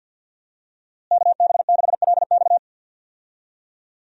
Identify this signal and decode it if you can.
Morse code, heard as a keyed tone, about 53 wpm, 700 Hz. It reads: KB6LX
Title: USA Call Signs - 5 Characters - Repeated 2x - 50wpm